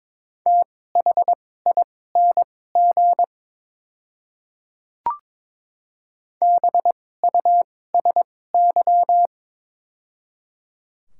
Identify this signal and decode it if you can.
Morse code, heard as a keyed tone, about 22 wpm, spaced as for 16 wpm, 700 Hz. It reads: THING BUSY
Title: 2x Character Spacing - Top 500 Words - 22wpm